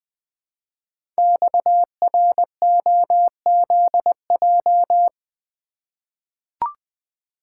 Morse code, heard as a keyed tone, about 20 wpm, 700 Hz. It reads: XROZJ
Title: Code Groups - 5 Letters - 20wpm